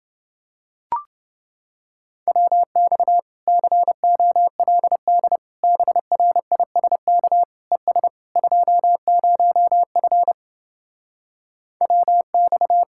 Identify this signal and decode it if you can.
Morse code, heard as a keyed tone, about 30 wpm, 700 Hz. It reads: WX COLD BRISK ES 20F WX